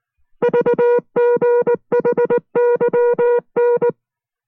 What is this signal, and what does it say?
Morse code, heard as a keyed tone, about 19 wpm, 480 Hz. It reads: VGHYN